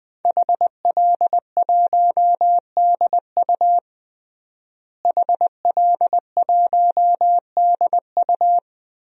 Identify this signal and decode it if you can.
Morse code, heard as a keyed tone, about 20 wpm, 700 Hz. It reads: HL1DU HL1DU
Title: DXCC-100 Call Signs - 5 Characters - Repeated 2x - 20wpm